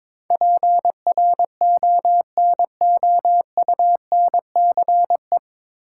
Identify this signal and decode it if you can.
Morse code, heard as a keyed tone, about 22 wpm, 700 Hz. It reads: PRONOUNCE
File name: 9 Letter Words - First 3 Letters Given - 22wpm